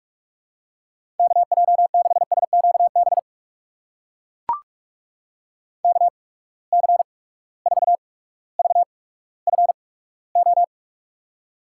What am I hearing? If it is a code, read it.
Morse code, heard as a keyed tone, about 44 wpm, spaced as for 19 wpm, 700 Hz. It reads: KJ6SQB KC4VFO
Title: USA Call Signs - 6 Characters - 8x Character Spacing - 45wpm